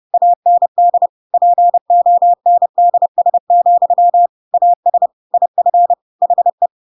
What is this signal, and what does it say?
Morse code, heard as a keyed tone, about 30 wpm, 700 Hz. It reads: AND PONDS, AS IF HE